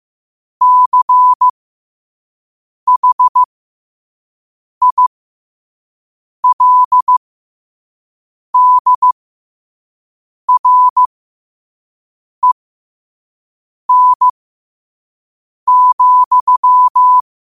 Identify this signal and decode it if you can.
Morse code, heard as a keyed tone, about 15 wpm, spaced as for 5 wpm, 1000 Hz. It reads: CHILDREN,
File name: qotdfiles / 5wpm